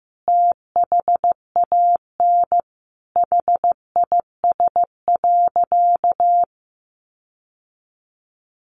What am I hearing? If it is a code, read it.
Morse code, heard as a keyed tone, about 15 wpm, 700 Hz. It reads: THAN HIS.
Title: Sentences from Top 1500 Words 15wpm